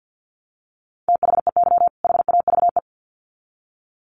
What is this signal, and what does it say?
Morse code, heard as a keyed tone, about 50 wpm, 700 Hz. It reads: THEY HAVE
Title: Sets of 2 Words 50wpm